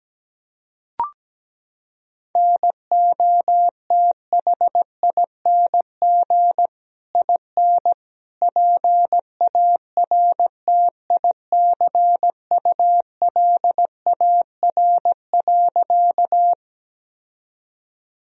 Morse code, heard as a keyed tone, about 17 wpm, 700 Hz. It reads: NOTHING IN PARTICULAR.
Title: Sentences from Top 1000 Words - Review Part 4 - 17wpm